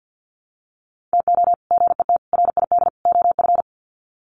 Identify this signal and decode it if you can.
Morse code, heard as a keyed tone, about 50 wpm, 700 Hz. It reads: TO GET RID OF
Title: Sets of 4 Words 50wpm